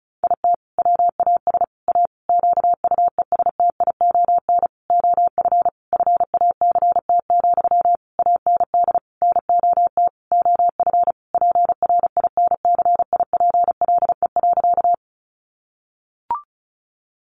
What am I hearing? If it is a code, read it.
Morse code, heard as a keyed tone, about 34 wpm, 700 Hz. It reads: IT WAS A QUESTION OF FACT, AND NOT OF PRINCIPLE.
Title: Sentences from Top 1400 Words 35wpm